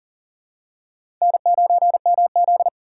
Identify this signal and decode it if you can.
Morse code, heard as a keyed tone, about 40 wpm, 700 Hz. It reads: N9MZ
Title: USA Call Signs - 4 Characters - 4x Character Spacing - 40wpm